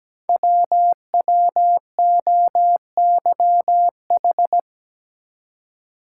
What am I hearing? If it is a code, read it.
Morse code, heard as a keyed tone, about 17 wpm, 700 Hz. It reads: WWOYH
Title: Code Groups - 5 Letters - 17wpm